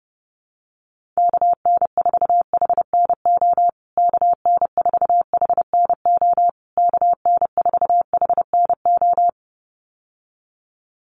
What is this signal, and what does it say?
Morse code, heard as a keyed tone, about 30 wpm, 700 Hz. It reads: KN4HNO KN4HNO KN4HNO
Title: USA Call Signs Repeated 3x 30wpm